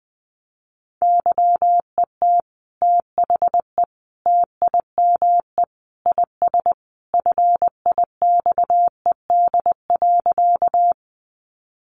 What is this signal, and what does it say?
Morse code, heard as a keyed tone, about 20 wpm, 700 Hz. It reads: YET THE TIME IS FIXED.
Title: Sentences from Top 1100 Words 20wpm